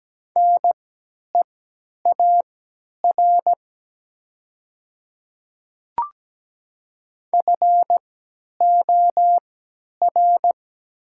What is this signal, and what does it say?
Morse code, heard as a keyed tone, about 17 wpm, spaced as for 10 wpm, 700 Hz. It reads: NEAR FOR
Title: 3x Character Spacing - Top 200 Words - 17wpm